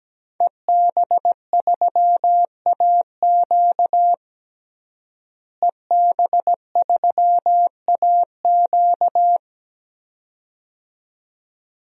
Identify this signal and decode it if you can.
Morse code, heard as a keyed tone, about 17 wpm, 700 Hz. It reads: EB3AQ EB3AQ
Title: DXCC-100 Call Signs - 5 Characters - Repeated 2x - 17wpm